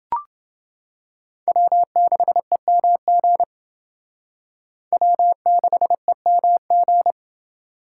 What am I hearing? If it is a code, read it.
Morse code, heard as a keyed tone, about 28 wpm, 700 Hz. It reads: W6EMG W6EMG